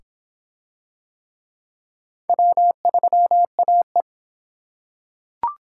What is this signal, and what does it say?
Morse code, heard as a keyed tone, about 21 wpm, 700 Hz. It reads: W3AE